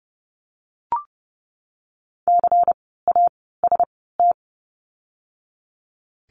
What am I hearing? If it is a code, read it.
Morse code, heard as a keyed tone, about 30 wpm, 700 Hz. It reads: C A S T
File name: Lesson 16f - Letter c - Words 30wpm